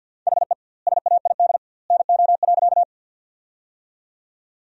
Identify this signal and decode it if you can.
Morse code, heard as a keyed tone, about 50 wpm, 700 Hz. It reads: HE SAID NO.
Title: 2x Word-Spacing - Sentences from Top 100 Words - 50wpm